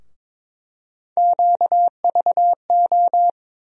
Morse code, heard as a keyed tone, about 21 wpm, 700 Hz. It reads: QVO
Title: USA Call Sign Postfixes - 3 Letters 22wpm